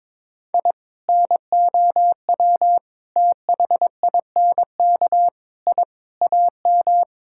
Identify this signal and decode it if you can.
Morse code, heard as a keyed tone, about 22 wpm, 700 Hz. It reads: I NOW THINK I AM
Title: Sentences from Top 1000 Words - Review Part 5 - 22wpm